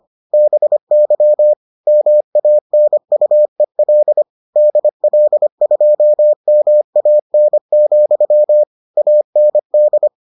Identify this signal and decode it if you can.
Morse code, heard as a keyed tone, about 25 wpm, 600 Hz. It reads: BY MANUEL DL2MAN, AND